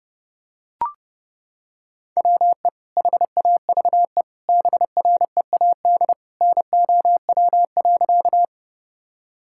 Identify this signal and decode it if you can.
Morse code, heard as a keyed tone, about 30 wpm, 700 Hz. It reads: WE HAVE BREAD NOW.
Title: Sentences from Top 1600 Words 30wpm